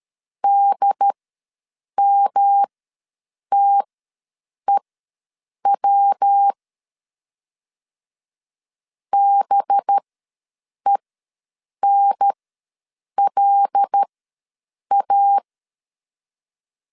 Morse code, heard as a keyed tone, about 13 wpm, spaced as for 7 wpm, 800 Hz. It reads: DMTEW BENLA
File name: LEZIONI / 5 LBVC / 5 LBV ws 12wpm 3015